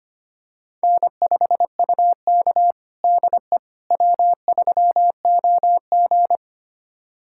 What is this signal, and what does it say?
Morse code, heard as a keyed tone, about 25 wpm, 700 Hz. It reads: N5UK DE W3OG
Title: USA Callsign DE Callsign - 4 Characters - 25wpm